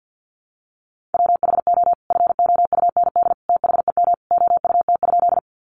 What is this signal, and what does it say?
Morse code, heard as a keyed tone, about 50 wpm, 700 Hz. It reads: WHO FOUND THEM OUT?
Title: Sentences from Top 300 Words 50wpm